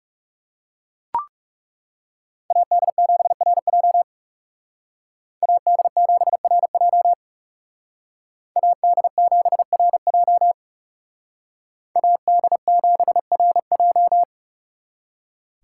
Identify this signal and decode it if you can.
Morse code, heard as a keyed tone, about 35 wpm, 700 Hz. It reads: AD7RJ AD7RJ AD7RJ AD7RJ
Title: Speed Racing - USA 2x2 Call Signs 30wpm